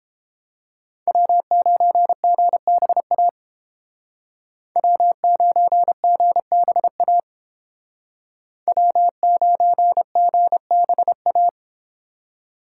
Morse code, heard as a keyed tone, about 30 wpm, 700 Hz. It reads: W9GBA W9GBA W9GBA